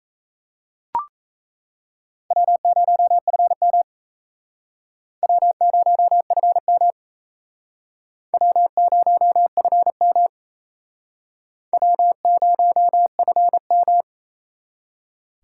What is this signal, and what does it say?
Morse code, heard as a keyed tone, about 35 wpm, 700 Hz. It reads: W0FM W0FM W0FM W0FM
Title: Speed Racing - USA 1x2 Call Signs 28wpm